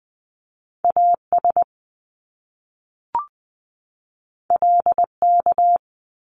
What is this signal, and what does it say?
Morse code, heard as a keyed tone, about 20 wpm, 700 Hz. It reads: AS LK
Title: Top Two Letter Combinations in Words 20wpm